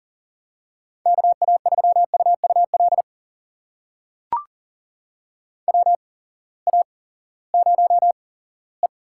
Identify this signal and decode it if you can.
Morse code, heard as a keyed tone, about 40 wpm, spaced as for 20 wpm, 700 Hz. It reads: KA3UUL WA0E
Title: USA Call Signs - 6 Characters - 8x Character Spacing - 40wpm